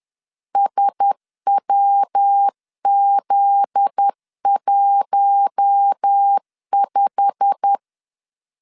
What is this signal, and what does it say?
Morse code, heard as a keyed tone, about 10 wpm, 800 Hz. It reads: SWZ15
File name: LEZIONI / 9 2 10wpm 2736